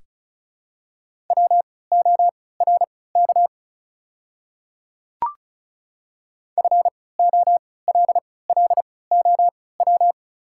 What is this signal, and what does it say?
Morse code, heard as a keyed tone, about 34 wpm, spaced as for 20 wpm, 700 Hz. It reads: WORK FOLLOW
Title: 3x Character Spacing - Top 300 Words - 35wpm